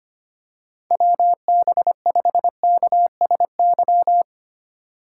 Morse code, heard as a keyed tone, about 25 wpm, 700 Hz. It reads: WB5KSY
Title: USA Call Signs - 8x Character Spacing - 25wpm